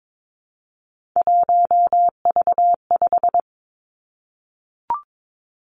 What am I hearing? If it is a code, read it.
Morse code, heard as a keyed tone, about 22 wpm, 700 Hz. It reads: 1V5
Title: Three Challenging Letter-Number Combinations 22wpm